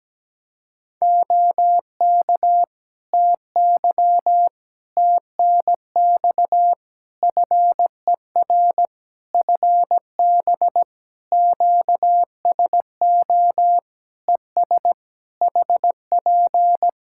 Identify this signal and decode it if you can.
Morse code, heard as a keyed tone, about 17 wpm, 700 Hz. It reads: OK TY TNX FER FB QSO ES HP
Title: K1USN SST QSO Practice 17wpm